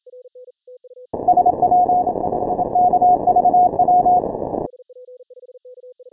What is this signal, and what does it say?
Morse code, heard as a keyed tone, about 26 wpm, 720 Hz, with noise. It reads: SW KVW
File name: final / W1UJ-168-140-MorseRunner